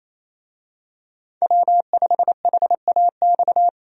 Morse code, heard as a keyed tone, about 27 wpm, 700 Hz. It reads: W5HAX
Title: USA Call Signs - 5 Characters - Repeated 3x - 28wpm